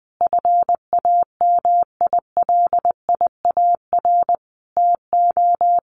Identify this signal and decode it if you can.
Morse code, heard as a keyed tone, about 20 wpm, 700 Hz. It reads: FAMILIAR TO
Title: Sentences from Top 1100 Words 20wpm